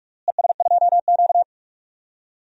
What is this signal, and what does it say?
Morse code, heard as a keyed tone, about 45 wpm, 700 Hz. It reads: ES2Q
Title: DXCC-100 Call Signs - 4 Characters - Repeated 2x - 45wpm